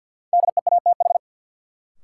Morse code, heard as a keyed tone, about 52 wpm, 700 Hz. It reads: DEATH
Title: Sentences from Top 1100 Words - New Word Drill Alt - 50wpm